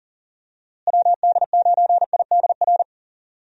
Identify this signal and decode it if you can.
Morse code, heard as a keyed tone, about 37 wpm, 700 Hz. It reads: WD9IDR
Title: USA Call Signs - 6 Characters - Repeated 2x - 40wpm